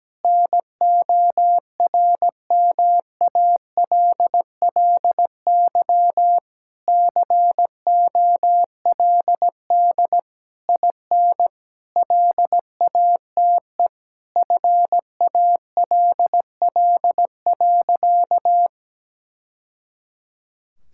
Morse code, heard as a keyed tone, about 17 wpm, 700 Hz. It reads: NORMALLY COLD IN LATE FALL.
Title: Sentences from Top 1000 Words - Review Part 6 - 17wpm